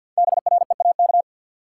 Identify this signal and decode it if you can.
Morse code, heard as a keyed tone, about 53 wpm, 700 Hz. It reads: BREAK
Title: Sets of 4 Words - New Word Drill - 50wpm